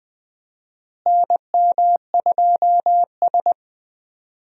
Morse code, heard as a keyed tone, about 19 wpm, 700 Hz. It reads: NM2S
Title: USA Call Signs - 4 Characters - Repeated 3x - 20wpm